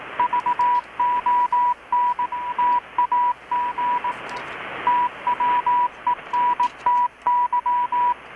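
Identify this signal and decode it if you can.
Morse code, heard as a keyed tone, about 18 wpm, 980 Hz, with radio noise. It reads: VON TAG TWENTY
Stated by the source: Morse from KPH's Night of Nights XX maritime radiotelegraphy event, received in Chiapas, Mexico about 3,800 km away on a Kenwood R-600 with a grounded 10 m long-wire antenna.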